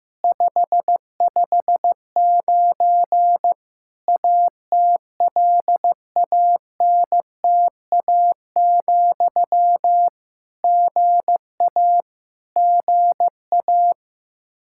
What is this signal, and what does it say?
Morse code, heard as a keyed tone, about 15 wpm, 700 Hz. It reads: 559 ATLANTA, GA GA
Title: K1USN SST QSO Practice 15wpm